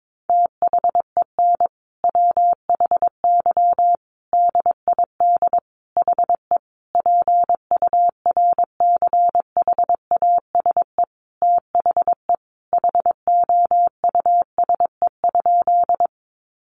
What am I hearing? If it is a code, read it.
Morse code, heard as a keyed tone, about 22 wpm, 700 Hz. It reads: THEN WHY DID HE PURCHASE THE HOUSE?
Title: Sentences from Top 800 Words 22wpm